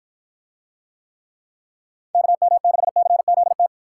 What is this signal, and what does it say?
Morse code, heard as a keyed tone, about 53 wpm, 700 Hz. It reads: KM6CZT